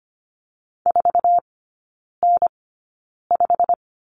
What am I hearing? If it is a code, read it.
Morse code, heard as a keyed tone, about 25 wpm, spaced as for 9 wpm, 700 Hz. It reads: 4N5
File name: Lesson 15e - Number 5 - Three Characters 25wpm